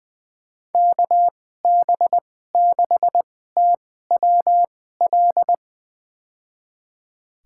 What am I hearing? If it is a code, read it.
Morse code, heard as a keyed tone, about 21 wpm, spaced as for 15 wpm, 700 Hz. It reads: KB6TWL
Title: USA Call Signs - 2x Character Spacing - 20wpm